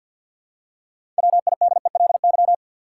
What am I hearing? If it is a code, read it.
Morse code, heard as a keyed tone, about 53 wpm, 700 Hz. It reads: WIDELY
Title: Six Letter Words - 50wpm